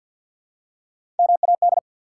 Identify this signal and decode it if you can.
Morse code, heard as a keyed tone, about 46 wpm, 700 Hz. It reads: MAD
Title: Three Letter Words - 50wpm